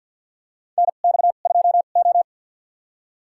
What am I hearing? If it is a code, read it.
Morse code, heard as a keyed tone, about 53 wpm, spaced as for 37 wpm, 700 Hz. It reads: NX2O
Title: USA Call Signs - 4 Characters - 2x Character Spacing - 50wpm